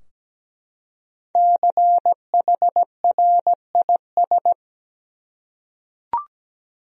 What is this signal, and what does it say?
Morse code, heard as a keyed tone, about 17 wpm, 700 Hz. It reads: CHRIS
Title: USA Common Names - Max 5 Letters - 17wpm